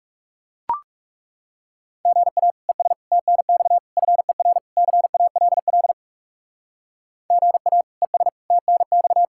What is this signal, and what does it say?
Morse code, heard as a keyed tone, about 36 wpm, 700 Hz. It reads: GA ES TNX FER CALL GA ES TNX
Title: Speed-Racing Common QSO Phrases Encore - 30wpm